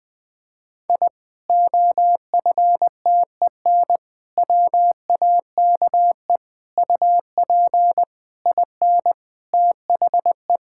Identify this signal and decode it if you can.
Morse code, heard as a keyed tone, about 20 wpm, 700 Hz. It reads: I OFTEN WAKE UP IN THE
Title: Sentences with 8-Word Phrases - Spoken Beginning - 20wpm